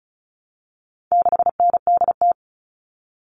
Morse code, heard as a keyed tone, about 34 wpm, 700 Hz. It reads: 6NDT
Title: All Cut Numbers - 4-Digit 35wpm